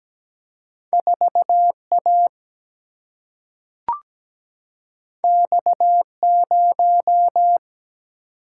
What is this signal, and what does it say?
Morse code, heard as a keyed tone, about 17 wpm, 700 Hz. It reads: 4A X0
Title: Single-Letter Single-Number 17wpm